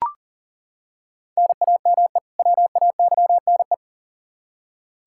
Morse code, heard as a keyed tone, about 40 wpm, 700 Hz. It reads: NAME WAYNE
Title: Practice QSO - Exchange 1 - Beginning - 40wpm